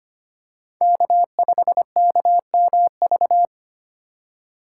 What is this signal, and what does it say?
Morse code, heard as a keyed tone, about 25 wpm, 700 Hz. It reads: K5KMV
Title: USA Call Signs - 5 Characters - 2x Character Spacing - 25wpm